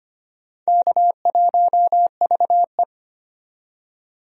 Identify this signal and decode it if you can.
Morse code, heard as a keyed tone, about 25 wpm, 700 Hz. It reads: K1VE
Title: USA Call Signs - 4 Characters - 25wpm